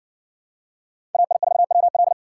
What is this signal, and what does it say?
Morse code, heard as a keyed tone, about 60 wpm, 700 Hz. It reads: AI4WL